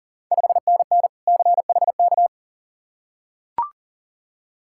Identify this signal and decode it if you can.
Morse code, heard as a keyed tone, about 40 wpm, 690 Hz. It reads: 5NN CHK
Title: Ontario QSO Party Exchange 40wpm